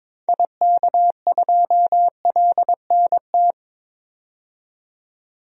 Morse code, heard as a keyed tone, about 22 wpm, 700 Hz. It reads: IK2LNT
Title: DXCC-100 Call Signs - 6 Characters - 22wpm